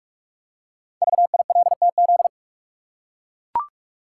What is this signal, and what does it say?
Morse code, heard as a keyed tone, about 44 wpm, 700 Hz. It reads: VILTZ
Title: Code Groups - 5 Letters - 45wpm